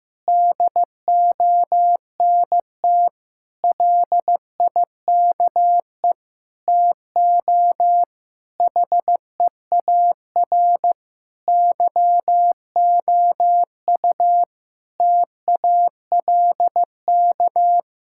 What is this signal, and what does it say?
Morse code, heard as a keyed tone, about 16 wpm, 700 Hz. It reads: DONT LIKE TO HEAR YOU TALK
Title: Sentences from Top 1000 Words - Review Part 6 - 15wpm